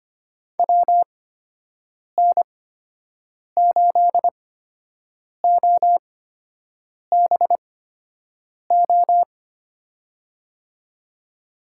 Morse code, heard as a keyed tone, about 25 wpm, spaced as for 7 wpm, 700 Hz. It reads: WN8OBO
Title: USA Call Signs - 6 Characters - 8x Character Spacing - 25wpm